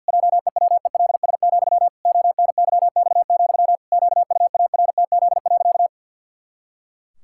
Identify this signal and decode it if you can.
Morse code, heard as a keyed tone, about 51 wpm, 700 Hz. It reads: JEWELS, ONYX, QUARTZ.
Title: Pangram Challenge v2 - 50wpm